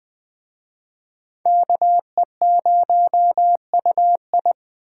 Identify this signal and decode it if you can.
Morse code, heard as a keyed tone, about 20 wpm, 700 Hz. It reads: KE0UI
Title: Speed Racing - USA Call Signs 17wpm